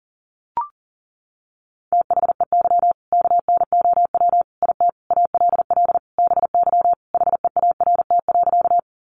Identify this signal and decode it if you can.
Morse code, heard as a keyed tone, about 40 wpm, 700 Hz. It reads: THEY KNOW IT ALL BY HEART.
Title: Sentences from Top 1200 Words 40wpm